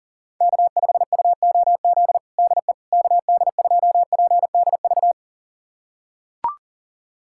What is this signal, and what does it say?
Morse code, heard as a keyed tone, about 40 wpm, 700 Hz. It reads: K5UOZ DE KD2PDV
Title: USA Callsign DE Callsign 40wpm